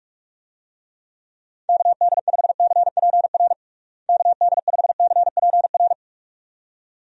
Morse code, heard as a keyed tone, about 41 wpm, 700 Hz. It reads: KD5CPR KD5CPR
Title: USA Call Signs - 6 Characters - Repeated 2x - 45wpm